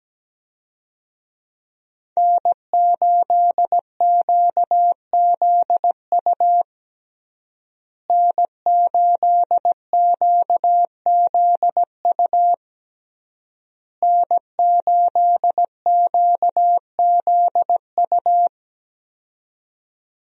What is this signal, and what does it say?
Morse code, heard as a keyed tone, about 17 wpm, 700 Hz. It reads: N8QZU N8QZU N8QZU
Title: USA Call Signs - 5 Characters - Repeated 3x - 17wpm